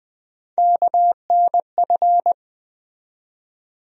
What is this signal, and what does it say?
Morse code, heard as a keyed tone, about 20 wpm, 700 Hz. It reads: KNF
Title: USA Call Sign Postfixes - 3 Letters 20wpm